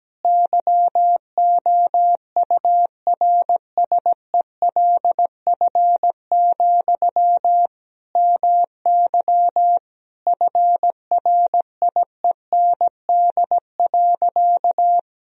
Morse code, heard as a keyed tone, about 17 wpm, 700 Hz. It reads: YOURSELF, MY FRIEND.
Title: Sentences from Top 1000 Words - Review Part 3 - 17wpm